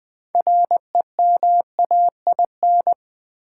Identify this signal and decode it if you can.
Morse code, heard as a keyed tone, about 20 wpm, 700 Hz. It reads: REMAIN
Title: Sentences from Top 1200 Words - New Word Drill Alt - 20wpm